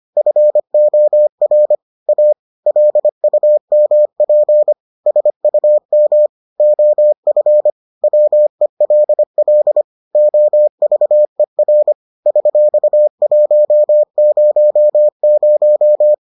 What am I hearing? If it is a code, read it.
Morse code, heard as a keyed tone, about 25 wpm, 600 Hz. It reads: FOR A LUMP SUM OF WELL OVER $100